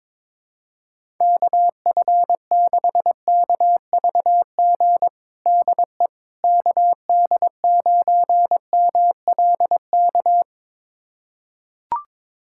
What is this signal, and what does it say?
Morse code, heard as a keyed tone, about 22 wpm, 700 Hz. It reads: KF6KVG DE KD9MLK
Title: USA Callsign DE Callsign 22wpm